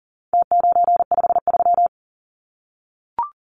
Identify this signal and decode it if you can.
Morse code, heard as a keyed tone, about 40 wpm, 700 Hz. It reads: T953
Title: Numbers - Contest Style - 4-Digit 40wpm